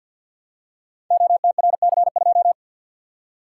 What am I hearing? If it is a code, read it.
Morse code, heard as a keyed tone, about 51 wpm, 700 Hz. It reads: OTRC2
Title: Code Groups - 5 Letter:Numbers - 50wpm